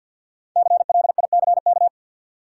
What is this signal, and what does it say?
Morse code, heard as a keyed tone, about 52 wpm, 700 Hz. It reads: CLICK
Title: Sentences from Top 1100 Words - New Word Drill - 50wpm